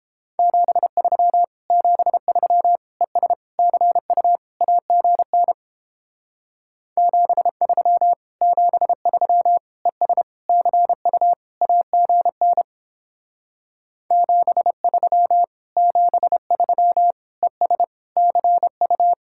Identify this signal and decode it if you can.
Morse code, heard as a keyed tone, about 29 wpm, 700 Hz. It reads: 73 73 ES CU AGN 73 73 ES CU AGN 73 73 ES CU